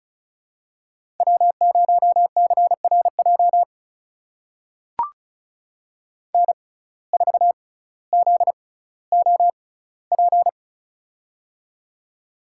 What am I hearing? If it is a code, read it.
Morse code, heard as a keyed tone, about 36 wpm, spaced as for 18 wpm, 700 Hz. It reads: W0CRJ N4ZOP
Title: USA Call Signs - 5 Characters - 6x Character Spacing - 35wpm